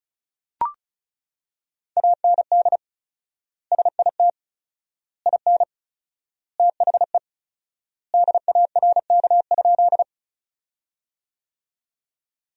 Morse code, heard as a keyed tone, about 34 wpm, 700 Hz. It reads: AND SIT IN THE DARK?
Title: 4x Word-Spacing - Sentences from Top 500 Words - 35wpm